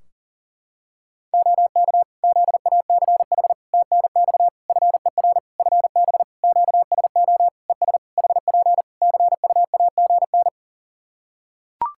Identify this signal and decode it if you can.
Morse code, heard as a keyed tone, about 37 wpm, 700 Hz. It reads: OK ZACH TNX FER FB QSO ES HP CUAGN
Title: K1USN SST QSO Practice 40wpm